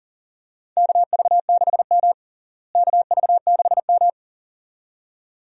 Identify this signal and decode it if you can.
Morse code, heard as a keyed tone, about 40 wpm, 700 Hz. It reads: KV6M KV6M
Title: USA Call Signs - 4 Characters - Repeated 2x - 40wpm